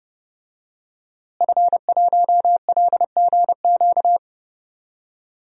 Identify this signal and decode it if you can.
Morse code, heard as a keyed tone, about 32 wpm, 700 Hz. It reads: F1LGQ
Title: DXCC-100 Call Signs - 5 Characters - Repeated 2x - 30wpm